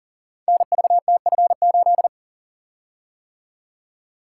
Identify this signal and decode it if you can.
Morse code, heard as a keyed tone, about 40 wpm, 700 Hz. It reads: NVTF8
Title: Code Groups - 5 Letter:Numbers - 40wpm